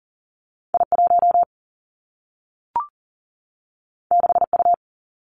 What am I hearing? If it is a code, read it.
Morse code, heard as a keyed tone, about 38 wpm, 700 Hz. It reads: I1 6U
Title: Two Challenging Letter-Number Combinations 40wpm